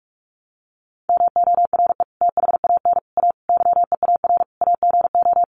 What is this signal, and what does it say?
Morse code, heard as a keyed tone, about 44 wpm, 690 Hz. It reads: MORE THAN A YEAR AGO
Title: Sets of 5 Words Encore2 45wpm